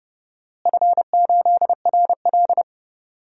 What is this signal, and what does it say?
Morse code, heard as a keyed tone, about 31 wpm, 700 Hz. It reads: F8RL
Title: DXCC-100 Call Signs - 4 Characters - Repeated 2x - 30wpm